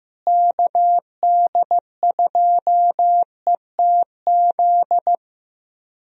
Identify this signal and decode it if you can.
Morse code, heard as a keyed tone, about 15 wpm, 700 Hz. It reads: KD2ETZ
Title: USA Call Signs - 6 Characters - Repeated 3x - 15wpm